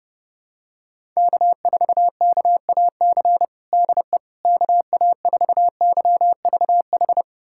Morse code, heard as a keyed tone, about 30 wpm, 700 Hz. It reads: K4KAC DE KA4YVH
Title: USA Callsign DE Callsign 30wpm